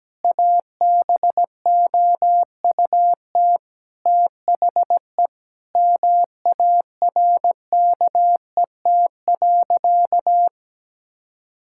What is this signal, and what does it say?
Morse code, heard as a keyed tone, about 17 wpm, 700 Hz. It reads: ABOUT THE MARKET.
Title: Sentences from Top 1000 Words - Review Part 5 - 17wpm